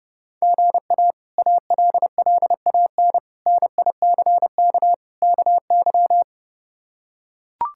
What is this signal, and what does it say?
Morse code, heard as a keyed tone, about 30 wpm, 700 Hz. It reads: GA ALLAN NICK KY
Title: K1USN SST QSO Practice - 30wpm